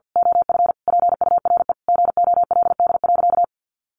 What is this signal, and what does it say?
Morse code, heard as a keyed tone, about 50 wpm, 700 Hz. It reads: OF PURE GOLD.
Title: Sentences from Top 1500 Words 50wpm